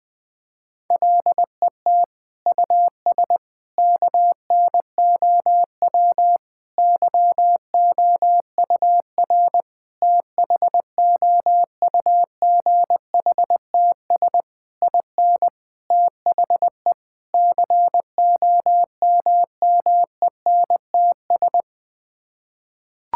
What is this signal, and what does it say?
Morse code, heard as a keyed tone, about 20 wpm, 700 Hz. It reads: LET US KNOW YOUR THOUGHTS IN THE COMMENTS
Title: Sets of 8 Words - 20wpm